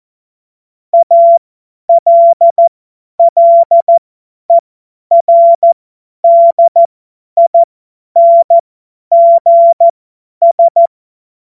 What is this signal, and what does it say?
Morse code, heard as a keyed tone, about 14 wpm, spaced as for 10 wpm, 680 Hz. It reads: ALLERDINGS
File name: Morsekurs / Witze / Witz 02. August 2022 50 F70